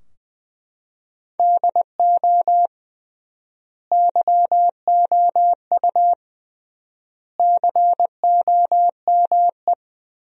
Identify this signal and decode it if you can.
Morse code, heard as a keyed tone, about 20 wpm, 700 Hz. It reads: DO YOU COME
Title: 3x Word-Spacing - Sentences from Top 200 Words - 20wpm